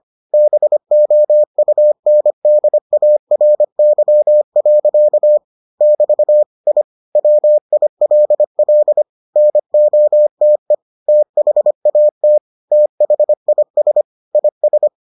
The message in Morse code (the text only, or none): BOUNDARY. <BT> I WILL NOTE THAT THIS IS